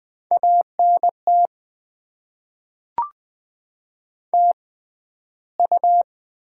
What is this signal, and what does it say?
Morse code, heard as a keyed tone, about 21 wpm, spaced as for 12 wpm, 700 Hz. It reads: ANT TU
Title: Top 20 QSO Elements - 6x Character Spacing - 20wpm